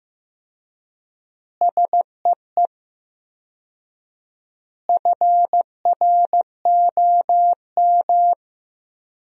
Morse code, heard as a keyed tone, about 15 wpm, 700 Hz. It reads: SEE FROM
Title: 4x Word-Spacing - Sentences from Top 100 Words - 15wpm